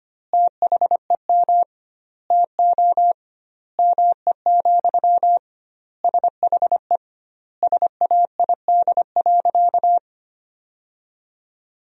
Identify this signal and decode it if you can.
Morse code, heard as a keyed tone, about 25 wpm, 700 Hz. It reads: THEM TO ME, SHE SAID.
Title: 2x Word-Spacing - Sentences from Top 500 Words - 25wpm